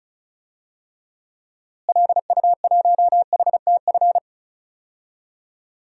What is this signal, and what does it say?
Morse code, heard as a keyed tone, about 36 wpm, 700 Hz. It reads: LU1HTF
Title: DXCC-100 Call Signs - 6 Characters - Repeated 2x - 35wpm